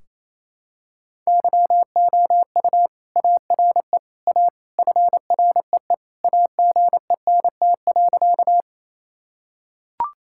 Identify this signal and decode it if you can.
Morse code, heard as a keyed tone, about 28 wpm, 700 Hz. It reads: YOU ARE A FREE AGENT.
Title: Sentences from Top 1600 Words 28wpm